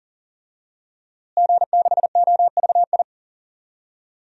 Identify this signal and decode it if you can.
Morse code, heard as a keyed tone, about 38 wpm, 700 Hz. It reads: G6OVI
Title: DXCC-100 Call Signs - 5 Characters - Repeated 2x - 40wpm